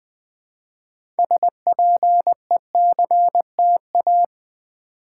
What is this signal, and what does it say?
Morse code, heard as a keyed tone, about 20 wpm, 700 Hz. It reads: SPECTA
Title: Build A Word - 3 Letters At A Time - 9-Letter Words - 20wpm